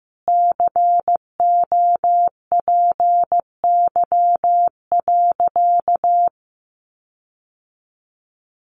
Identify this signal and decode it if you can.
Morse code, heard as a keyed tone, about 15 wpm, 700 Hz. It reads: COPY.
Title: Sentences from Top 1400 Words 15wpm